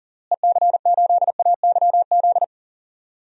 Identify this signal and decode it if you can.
Morse code, heard as a keyed tone, about 40 wpm, 700 Hz. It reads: EC8AYZ
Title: DXCC-100 Call Signs - 6 Characters - Repeated 3x - 40wpm